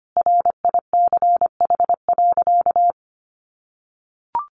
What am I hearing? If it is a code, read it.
Morse code, heard as a keyed tone, about 25 wpm, 700 Hz. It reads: RICH.
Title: Sentences from Top 1100 Words 25wpm